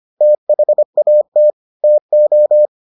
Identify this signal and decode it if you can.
Morse code, heard as a keyed tone, about 25 wpm, 600 Hz. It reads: THAT TO